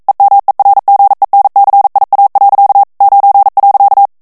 Morse code, heard as a keyed tone, about 42 wpm, 800 Hz. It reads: EMERGENCIA. 9.
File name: morse mp3 / ESPAGNOL TEXTE EN CLAIR 1 VITESSE 40 MOTS MINUTES